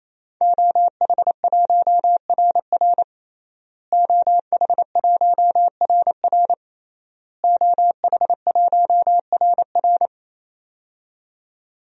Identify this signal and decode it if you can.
Morse code, heard as a keyed tone, about 28 wpm, 690 Hz. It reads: OH1RR OH1RR OH1RR
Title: DXCC-100 Call Signs - 5 Characters - Repeated 3x - 28wpm